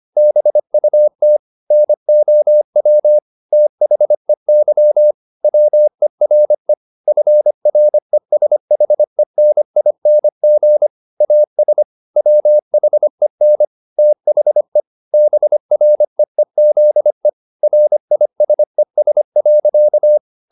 Morse code, heard as a keyed tone, about 25 wpm, 600 Hz. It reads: BUT NOW THEY WERE FRESHENING AS WHEN THE BREEZE RISES.
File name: the old man and the sea part 1 25wpm